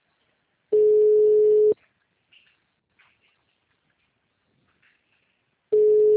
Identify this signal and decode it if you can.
Ringback tone (Europe, 425 Hz).